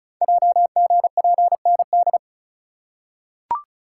Morse code, heard as a keyed tone, about 33 wpm, 700 Hz. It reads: JGPND
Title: Code Groups - 5 Letters - 35wpm